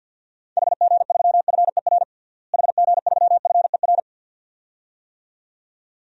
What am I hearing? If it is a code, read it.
Morse code, heard as a keyed tone, about 51 wpm, 700 Hz. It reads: HG3FER HG3FER
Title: DXCC-100 Call Signs - 6 Characters - Repeated 2x - 50wpm